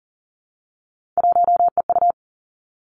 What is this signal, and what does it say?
Morse code, heard as a keyed tone, about 38 wpm, 700 Hz. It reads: 1EU
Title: Three Challenging Letter-Number Combinations 40wpm